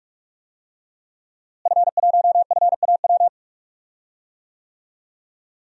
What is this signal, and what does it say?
Morse code, heard as a keyed tone, about 41 wpm, 700 Hz. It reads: F1FAW
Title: DXCC-100 Call Signs - 5 Characters - Repeated 3x - 45wpm